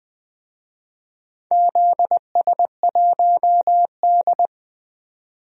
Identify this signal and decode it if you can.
Morse code, heard as a keyed tone, about 20 wpm, 700 Hz. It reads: ZS1D
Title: DXCC-100 Call Signs - 4 Characters - 20wpm